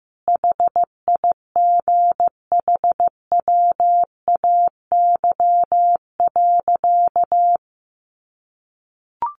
Morse code, heard as a keyed tone, about 15 wpm, 700 Hz. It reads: HIGHWAY.
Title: Sentences from Top 1400 Words 15wpm